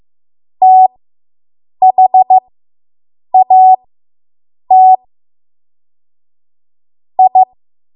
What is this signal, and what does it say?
Morse code, heard as a keyed tone, about 15 wpm, spaced as for 7 wpm, 750 Hz. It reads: THAT I